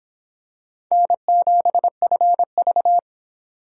Morse code, heard as a keyed tone, about 25 wpm, 700 Hz. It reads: N7FV